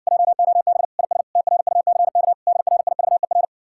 Morse code, heard as a keyed tone, about 60 wpm, 700 Hz. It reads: JOB IS TRUCK DRIVER